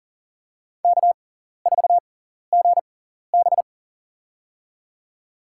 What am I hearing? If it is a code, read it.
Morse code, heard as a keyed tone, about 40 wpm, spaced as for 14 wpm, 700 Hz. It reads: K4GB
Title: USA Call Signs - 4 Characters - 6x Character Spacing - 40wpm